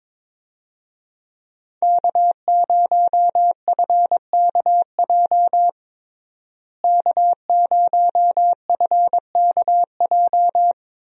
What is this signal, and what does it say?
Morse code, heard as a keyed tone, about 21 wpm, 700 Hz. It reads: K0FKJ K0FKJ
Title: USA Call Signs - 5 Characters - Repeated 2x - 22wpm